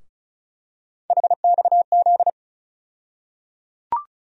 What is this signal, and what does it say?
Morse code, heard as a keyed tone, about 34 wpm, 700 Hz. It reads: HXZ
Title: USA Call Sign Postfixes - 3 Letters 35wpm